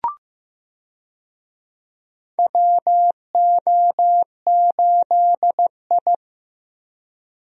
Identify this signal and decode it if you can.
Morse code, heard as a keyed tone, about 15 wpm, 700 Hz. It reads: WO8I
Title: USA Call Signs - 4 Characters - Repeated 3x - 15wpm